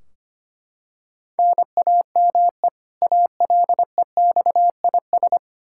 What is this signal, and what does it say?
Morse code, heard as a keyed tone, about 25 wpm, 700 Hz. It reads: NAME ALEXIS
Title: Practice QSO - Exchange 1 - Beginning - 25wpm